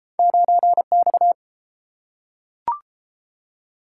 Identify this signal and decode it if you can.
Morse code, heard as a keyed tone, about 30 wpm, 700 Hz. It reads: 9X